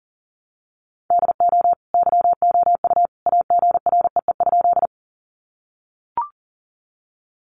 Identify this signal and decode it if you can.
Morse code, heard as a keyed tone, about 40 wpm, 700 Hz. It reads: DO YOU AGREE?
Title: Sentences from Top 1100 Words 40wpm